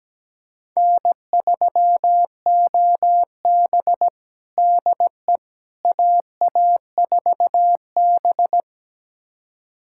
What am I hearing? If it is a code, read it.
Morse code, heard as a keyed tone, about 17 wpm, 700 Hz. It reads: N3OB DE AA4B
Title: USA Callsign DE Callsign - 4 Characters - 17wpm